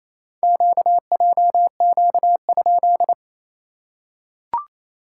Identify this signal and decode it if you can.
Morse code, heard as a keyed tone, about 28 wpm, 690 Hz. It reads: QJQ?
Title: SOTA QSO Practice 28wpm